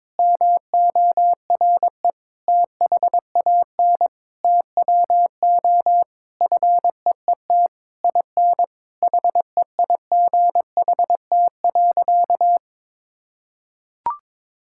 Morse code, heard as a keyed tone, about 22 wpm, 700 Hz. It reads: MORE THAN TWO FEET IN HEIGHT.
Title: Sentences from Top 1600 Words 22wpm